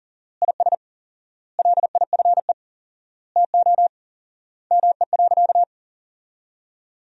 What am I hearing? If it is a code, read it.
Morse code, heard as a keyed tone, about 40 wpm, 700 Hz. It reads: IS LIFE TO ME.
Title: 4x Word-Spacing - Sentences from Top 500 Words - 40wpm